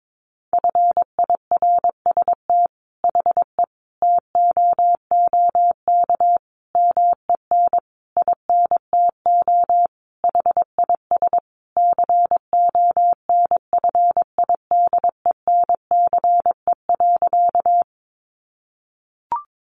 Morse code, heard as a keyed tone, about 22 wpm, 700 Hz. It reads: FIRST HE TOOK MEN INTO HIS CONFIDENCE.
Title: Sentences from Top 1500 Words 22wpm